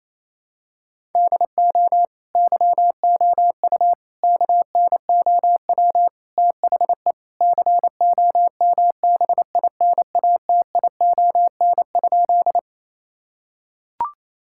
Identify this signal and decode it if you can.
Morse code, heard as a keyed tone, about 28 wpm, 700 Hz. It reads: DO YOU KNOW THE COMBINATION?
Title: Sentences from Top 1000 Words - Review Part 6 - 28wpm